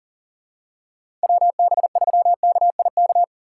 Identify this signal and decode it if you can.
Morse code, heard as a keyed tone, about 40 wpm, 700 Hz. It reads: WB3KIK
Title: USA Call Signs - 6 Characters - 3x Character Spacing - 40wpm